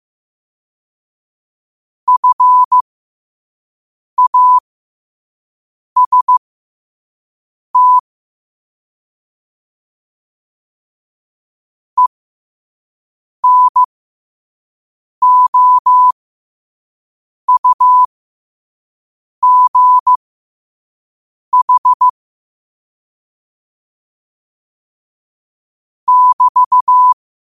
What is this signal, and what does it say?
Morse code, heard as a keyed tone, about 15 wpm, spaced as for 5 wpm, 1000 Hz. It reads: FAST ENOUGH <BT>
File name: qotdfiles / 5wpm